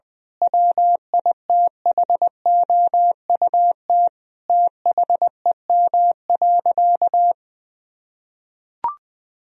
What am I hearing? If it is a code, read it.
Morse code, heard as a keyed tone, about 20 wpm, 700 Hz. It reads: WITHOUT THEM.
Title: Sentences from Top 1000 Words - Review Part 3 - 20wpm